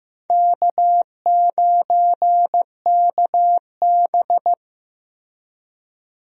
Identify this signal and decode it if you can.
Morse code, heard as a keyed tone, about 15 wpm, 700 Hz. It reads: K9KB
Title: USA Call Signs - 4 Characters - Repeated 3x - 15wpm